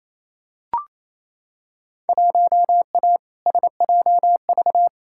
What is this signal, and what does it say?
Morse code, heard as a keyed tone, about 28 wpm, 700 Hz. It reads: 1A SJV
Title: Field Day Exchanges Encore - 28wpm